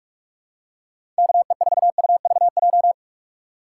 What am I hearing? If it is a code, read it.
Morse code, heard as a keyed tone, about 47 wpm, 700 Hz. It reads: KE4UVJ
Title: USA Call Signs - 6 Characters - 4x Character Spacing - 45wpm